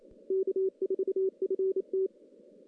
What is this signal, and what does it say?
Morse code, heard as a keyed tone, about 27 wpm, 380 Hz. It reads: K4FT
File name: contests / CW-SS-k5tr- -1600